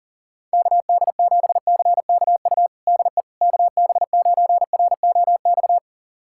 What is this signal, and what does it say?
Morse code, heard as a keyed tone, about 40 wpm, 700 Hz. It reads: KD7CKU DE KB9ROX
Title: USA Callsign DE Callsign 40wpm